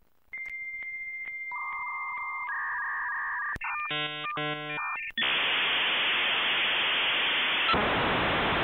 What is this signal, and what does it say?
Dial-up modem handshake.
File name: kandi-hippie kandi-hippie---dial-up-glitchtune